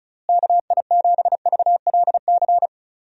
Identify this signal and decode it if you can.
Morse code, heard as a keyed tone, about 36 wpm, 700 Hz. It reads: KI7VLC
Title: USA Call Signs - 6 Characters - Repeated 3x - 35wpm